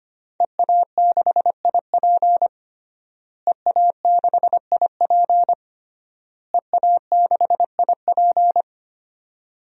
Morse code, heard as a keyed tone, about 25 wpm, 700 Hz. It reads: EA6IP EA6IP EA6IP
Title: DXCC-100 Call Signs - 5 Characters - Repeated 3x - 25wpm